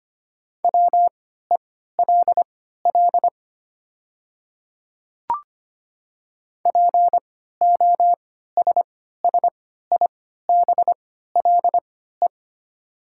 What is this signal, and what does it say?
Morse code, heard as a keyed tone, about 25 wpm, spaced as for 14 wpm, 700 Hz. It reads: WELL POSSIBLE
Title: 3x Character Spacing - Top 500 Words - 25wpm